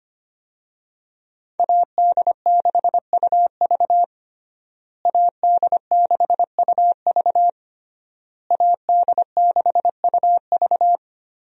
Morse code, heard as a keyed tone, about 25 wpm, 700 Hz. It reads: AD6UV AD6UV AD6UV
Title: USA Call Signs - 5 Characters - Repeated 3x - 25wpm